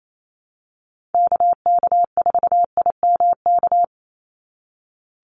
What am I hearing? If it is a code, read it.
Morse code, heard as a keyed tone, about 28 wpm, 700 Hz. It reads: KK4IMK
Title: USA Call Signs 28wpm